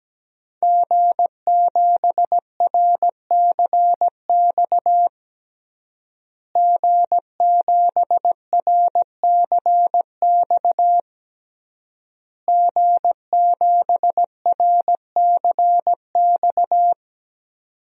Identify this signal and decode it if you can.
Morse code, heard as a keyed tone, about 17 wpm, 700 Hz. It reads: G7RCX G7RCX G7RCX
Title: DXCC-100 Call Signs - 5 Characters - Repeated 3x - 17wpm